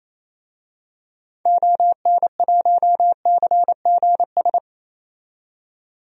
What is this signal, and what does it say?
Morse code, heard as a keyed tone, about 28 wpm, 700 Hz. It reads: ON1CGS
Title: DXCC-100 Call Signs - 6 Characters - 28wpm